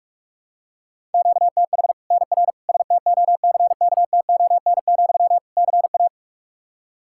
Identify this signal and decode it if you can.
Morse code, heard as a keyed tone, about 46 wpm, 700 Hz. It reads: QTH NR STOCKTON, CA
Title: Practice QSO - Exchange 1 - Beginning - 45wpm